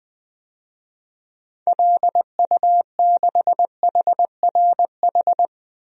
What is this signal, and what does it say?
Morse code, heard as a keyed tone, about 20 wpm, 700 Hz. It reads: LU6HRH
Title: DXCC-100 Call Signs - 6 Characters - 20wpm